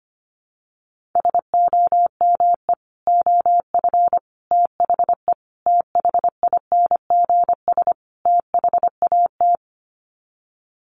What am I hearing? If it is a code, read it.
Morse code, heard as a keyed tone, about 25 wpm, 700 Hz. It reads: SOME OF THE THINGS THAT
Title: Sets of 5 Words 25wpm